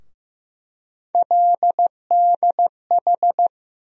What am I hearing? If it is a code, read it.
Morse code, heard as a keyed tone, about 15 wpm, 700 Hz. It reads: LDH
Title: USA Call Sign Postfixes - 3 Letters 15wpm